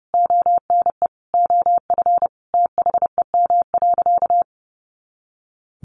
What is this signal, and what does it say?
Morse code, heard as a keyed tone, about 30 wpm, 700 Hz. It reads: ONE OF THEM.
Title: Sentences from Top 300 Words 30wpm